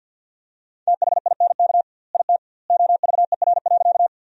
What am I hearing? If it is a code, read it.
Morse code, heard as a keyed tone, about 50 wpm, 700 Hz. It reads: THINK IT OVER.
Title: 2x Word-Spacing - Sentences from Top 200 Words - 50wpm